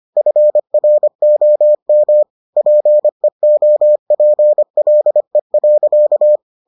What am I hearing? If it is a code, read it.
Morse code, heard as a keyed tone, about 25 wpm, 600 Hz. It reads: FROM PEOPLE.